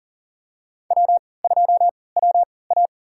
Morse code, heard as a keyed tone, about 40 wpm, spaced as for 23 wpm, 700 Hz. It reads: W2WA
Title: USA Call Signs - 4 Characters - 3x Character Spacing - 40wpm